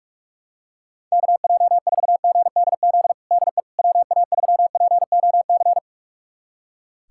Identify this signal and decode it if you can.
Morse code, heard as a keyed tone, about 46 wpm, 700 Hz. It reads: KJ4GDZ DE WA3POC
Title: USA Callsign DE Callsign 45wpm